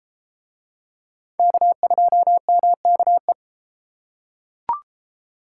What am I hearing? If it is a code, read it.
Morse code, heard as a keyed tone, about 33 wpm, 700 Hz. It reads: K2MKE